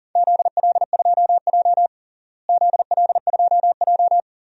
Morse code, heard as a keyed tone, about 40 wpm, 700 Hz. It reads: ZL2J ZL2J
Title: DXCC-100 Call Signs - 4 Characters - Repeated 3x - 40wpm